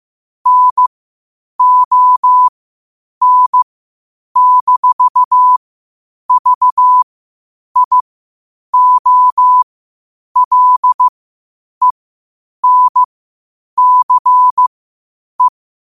Morse code, heard as a keyed tone, about 15 wpm, spaced as for 8 wpm, 1000 Hz. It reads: NON-VIOLENCE
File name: qotdfiles / 8wpm